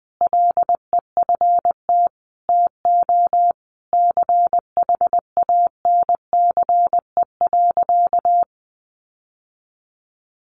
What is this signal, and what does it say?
Morse code, heard as a keyed tone, about 20 wpm, 700 Hz. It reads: LEFT TO CHANCE.
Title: Sentences from Top 1400 Words 20wpm